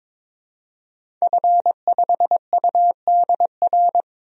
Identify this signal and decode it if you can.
Morse code, heard as a keyed tone, about 22 wpm, 700 Hz. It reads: F5UDR
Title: DXCC-100 Call Signs - 5 Characters - 22wpm